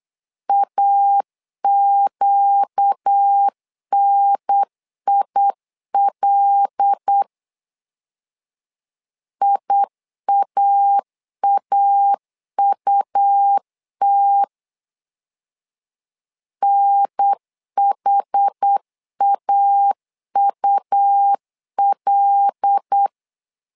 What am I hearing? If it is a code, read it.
Morse code, heard as a keyed tone, about 8 wpm, 800 Hz. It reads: AQNIL IAAUT NHAUL
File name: LEZIONI / 5 LBVC / 5 LB 8wpm 2816